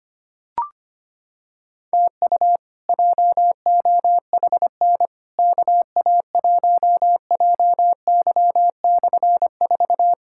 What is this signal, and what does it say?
Morse code, heard as a keyed tone, about 25 wpm, 700 Hz. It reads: TU JOHN KA1JY/4
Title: K1USN SST QSO Practice - 25wpm